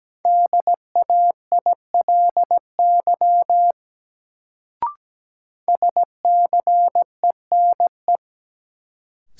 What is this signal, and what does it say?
Morse code, heard as a keyed tone, about 17 wpm, 700 Hz. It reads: DAILY SCENE
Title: Sentences from Top 1100 Words - New Word Drill - 17wpm